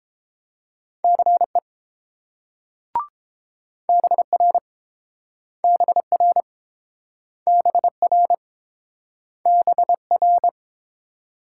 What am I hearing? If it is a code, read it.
Morse code, heard as a keyed tone, about 30 wpm, 700 Hz. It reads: CE BR BR BR BR